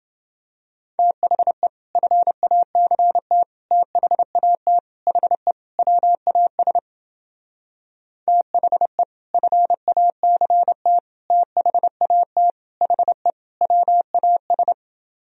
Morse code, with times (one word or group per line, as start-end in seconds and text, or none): 1.0-1.7 THE
2.0-3.4 FACT
3.7-4.8 THAT
5.1-5.5 HE
5.8-6.8 WAS
8.3-9.0 THE
9.3-11.0 FACT
11.3-12.5 THAT
12.8-13.3 HE
13.6-14.7 WAS